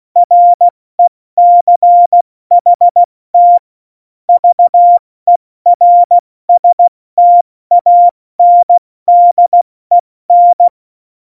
Morse code, heard as a keyed tone, about 16 wpm, 700 Hz. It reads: RECHT VERSTANDEN
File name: Morsekurs / Witze / Witz 16. Januar 2018, 70 F80